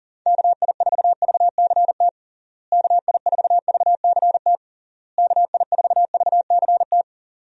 Morse code, heard as a keyed tone, about 40 wpm, 700 Hz. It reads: KI4VCT KI4VCT KI4VCT
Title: USA Call Signs - 6 Characters - Repeated 3x - 40wpm